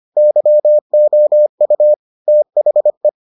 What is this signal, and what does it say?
Morse code, heard as a keyed tone, about 25 wpm, 600 Hz. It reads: YOU THE